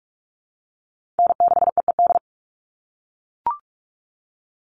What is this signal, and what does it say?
Morse code, heard as a keyed tone, about 44 wpm, 700 Hz. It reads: N6EED
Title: Call Signs - 45wpm